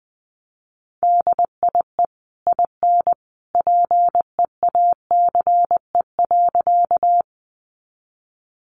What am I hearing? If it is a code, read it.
Morse code, heard as a keyed tone, about 20 wpm, 700 Hz. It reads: DIE IN PEACE.
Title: Sentences from Top 1500 Words 20wpm